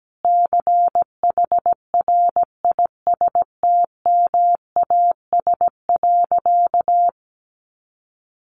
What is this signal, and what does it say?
Morse code, heard as a keyed tone, about 17 wpm, 700 Hz. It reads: CHRISTMAS.
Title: Easy Phrases 17wpm